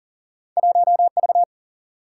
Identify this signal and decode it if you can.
Morse code, heard as a keyed tone, about 39 wpm, 700 Hz. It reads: 1V
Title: Single-Letter Single-Number 40wpm